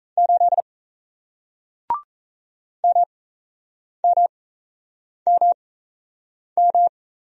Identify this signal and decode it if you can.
Morse code, heard as a keyed tone, about 37 wpm, spaced as for 8 wpm, 700 Hz. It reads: 8 MMMM